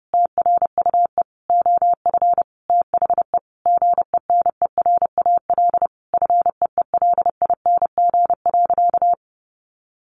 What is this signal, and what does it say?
Morse code, heard as a keyed tone, about 30 wpm, 700 Hz. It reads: TRUE OF THE GENERAL FEELING.
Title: Sentences from Top 1400 Words 30wpm